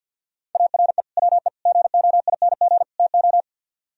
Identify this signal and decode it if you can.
Morse code, heard as a keyed tone, about 57 wpm, 700 Hz. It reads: ARE WE GOING TO
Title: Speed-Racing - Sets of 4 Words 50wpm